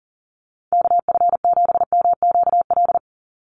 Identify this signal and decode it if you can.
Morse code, heard as a keyed tone, about 40 wpm, 700 Hz. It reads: KF7MQL
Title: USA Call Signs 40wpm